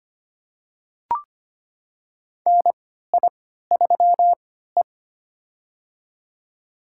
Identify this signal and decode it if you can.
Morse code, heard as a keyed tone, about 25 wpm, spaced as for 14 wpm, 700 Hz. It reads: NI3E
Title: USA Call Signs - 4 Characters - 3x Character Spacing - 25wpm